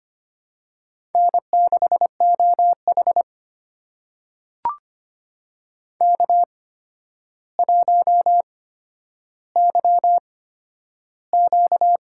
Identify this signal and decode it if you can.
Morse code, heard as a keyed tone, about 25 wpm, spaced as for 11 wpm, 700 Hz. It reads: N6OH K1YQ
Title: USA Call Signs - 4 Characters - 8x Character Spacing - 25wpm